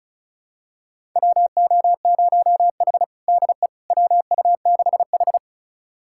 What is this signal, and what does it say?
Morse code, heard as a keyed tone, about 34 wpm, 700 Hz. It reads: WO0H DE WU6H
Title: USA Callsign DE Callsign - 4 Characters - 35wpm